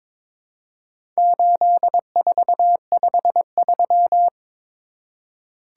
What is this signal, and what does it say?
Morse code, heard as a keyed tone, about 22 wpm, 700 Hz. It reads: 8453
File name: practice / 22wpm / Four Letter-Number Combinations - 22wpm